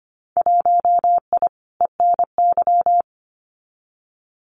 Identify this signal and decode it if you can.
Morse code, heard as a keyed tone, about 25 wpm, 700 Hz. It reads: 1I ENY
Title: Winter Field Day Exchanges Encore 25wpm